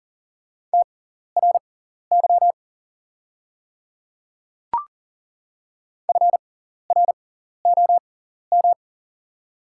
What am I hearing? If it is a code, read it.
Morse code, heard as a keyed tone, about 40 wpm, spaced as for 14 wpm, 700 Hz. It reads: TRY FROM
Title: 6x Character Spacing - Top 500 Words - 40wpm